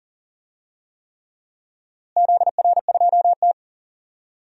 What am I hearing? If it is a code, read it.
Morse code, heard as a keyed tone, about 40 wpm, 700 Hz. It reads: ZR2T
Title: DXCC-100 Call Signs - 4 Characters - Repeated 3x - 40wpm